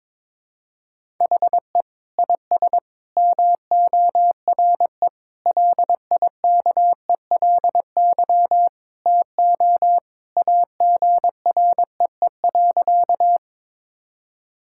Morse code, heard as a keyed tone, about 22 wpm, 700 Hz. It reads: HE IS MORE LIKELY TO AGREE.
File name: practice / 22wpm / Sentences with 3-Word Phrases - 22wpm